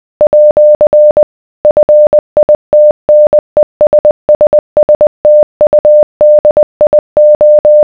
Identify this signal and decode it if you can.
Morse code, heard as a keyed tone, about 20 wpm, 600 Hz. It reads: @ FITNESSSTUDIO